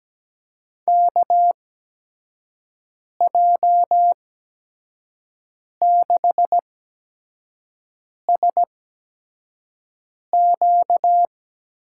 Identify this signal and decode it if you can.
Morse code, heard as a keyed tone, about 17 wpm, spaced as for 5 wpm, 700 Hz. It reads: KJ6SQ
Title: USA Call Signs - 6 Characters - 8x Character Spacing - 17wpm